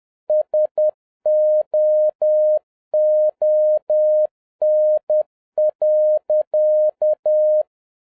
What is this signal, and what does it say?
Morse code, heard as a keyed tone, about 10 wpm, 610 Hz. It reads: SOON.